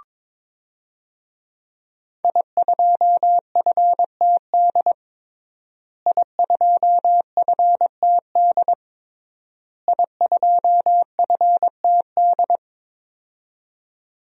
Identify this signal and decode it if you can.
Morse code, heard as a keyed tone, about 23 wpm, 700 Hz. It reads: I2FTD I2FTD I2FTD
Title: DXCC-100 Call Signs - 5 Characters - Repeated 3x - 22wpm